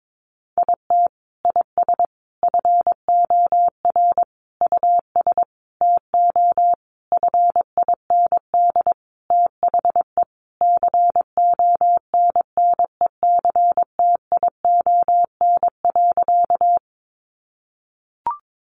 Morse code, heard as a keyed tone, about 22 wpm, 700 Hz. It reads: IT IS FOR US TO FIND THE CONNECTION.
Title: Sentences from Top 1300 Words 22wpm